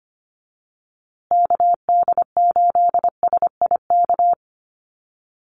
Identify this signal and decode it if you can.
Morse code, heard as a keyed tone, about 25 wpm, 700 Hz. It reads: KD8SIK
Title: USA Call Signs Repeated 3x 25wpm